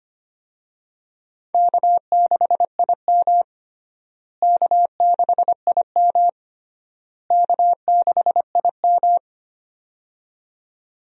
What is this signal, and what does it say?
Morse code, heard as a keyed tone, about 25 wpm, 700 Hz. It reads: K6IM K6IM K6IM
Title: USA Call Signs - 4 Characters - Repeated 3x - 25wpm